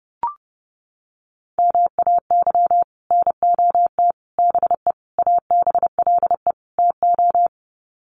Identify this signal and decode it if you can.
Morse code, heard as a keyed tone, about 28 wpm, 700 Hz. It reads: MAY NOT BE ABLE TO
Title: Sets of 5 Words 30wpm